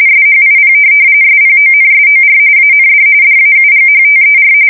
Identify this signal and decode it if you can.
RTTY, about 45.45 baud. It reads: SHIP SPONSOR AND DAUGHTER
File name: fsk170 / USS-Missouri